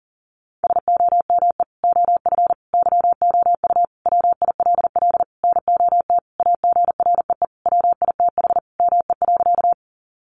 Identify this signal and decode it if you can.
Morse code, heard as a keyed tone, about 40 wpm, 700 Hz. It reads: SOME OF YOU WILL NOT AGREE WITH ME.
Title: Sentences from Top 1100 Words 40wpm